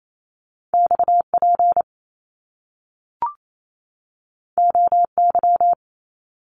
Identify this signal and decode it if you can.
Morse code, heard as a keyed tone, about 28 wpm, 700 Hz. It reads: XP OY
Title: Top Two Letter Combinations in Words 28wpm